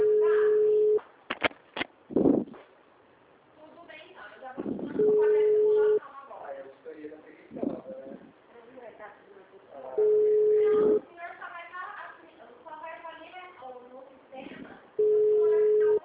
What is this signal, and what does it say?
Ringback tone (Europe, 425 Hz), with speech.
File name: out- -2002- - - .9565